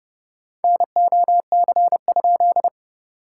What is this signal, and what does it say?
Morse code, heard as a keyed tone, about 30 wpm, 660 Hz. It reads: NOC?
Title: SOTA QSO Practice 30wpm